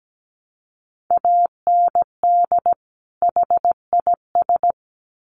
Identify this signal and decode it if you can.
Morse code, heard as a keyed tone, about 17 wpm, 700 Hz. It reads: AND HIS
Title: Sets of 2 Words 17wpm